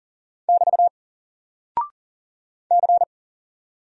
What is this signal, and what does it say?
Morse code, heard as a keyed tone, about 39 wpm, 700 Hz. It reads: <BT> C
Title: Single Character (Letter:Number:Punctuation:Prosign) Alt 40wpm